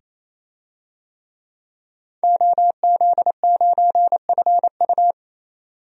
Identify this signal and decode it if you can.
Morse code, heard as a keyed tone, about 27 wpm, 700 Hz. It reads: OZ9FU
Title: DXCC-100 Call Signs - 5 Characters - 28wpm